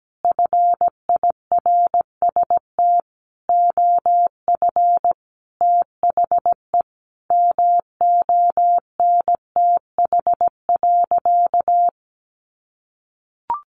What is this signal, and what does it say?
Morse code, heard as a keyed tone, about 17 wpm, 700 Hz. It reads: FIRST OF THE MONTH.
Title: Sentences from Top 800 Words 17wpm